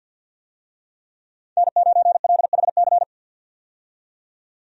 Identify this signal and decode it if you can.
Morse code, heard as a keyed tone, about 50 wpm, 700 Hz. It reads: N9LHC
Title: USA Call Signs - 5 Characters - 50wpm